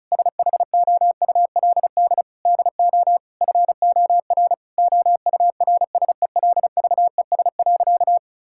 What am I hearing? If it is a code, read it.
Morse code, heard as a keyed tone, about 34 wpm, 700 Hz. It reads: SHOULD DO FOR OURSELVES.
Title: Sentences from Top 1600 Words 35wpm